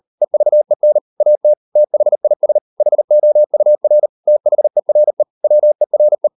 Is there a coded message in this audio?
Yes: Morse code from 0.2 to 6.3 s, heard as a keyed tone, about 38 wpm, 600 Hz. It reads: EVEN AT THIS HOUR THERE WERE